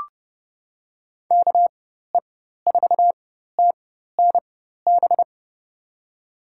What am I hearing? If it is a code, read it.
Morse code, heard as a keyed tone, about 30 wpm, spaced as for 14 wpm, 700 Hz. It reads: KE4TNB
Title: USA Call Signs - 6 Characters - 4x Character Spacing - 30wpm